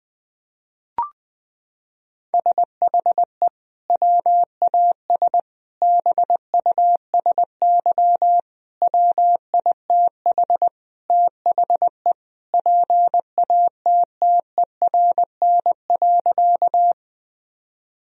Morse code, heard as a keyed tone, about 20 wpm, 700 Hz. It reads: SHE WAS BUSY WITH THE PATTERN.
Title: Sentences from Top 1000 Words - Review Part 2 - 20wpm